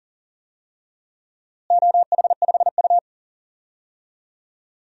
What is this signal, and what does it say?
Morse code, heard as a keyed tone, about 40 wpm, 700 Hz. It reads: OH5U
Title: DXCC-100 Call Signs - 4 Characters - Repeated 3x - 40wpm